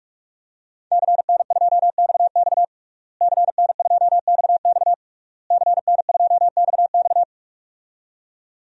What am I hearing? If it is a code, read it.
Morse code, heard as a keyed tone, about 43 wpm, 700 Hz. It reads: CN2XX CN2XX CN2XX
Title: DXCC-100 Call Signs - 5 Characters - Repeated 3x - 45wpm